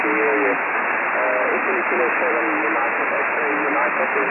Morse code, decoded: none